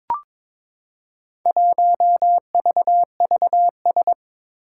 Morse code, heard as a keyed tone, about 22 wpm, 700 Hz. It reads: 1VVS
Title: Four Challenging Letter-Number Combinations 22wpm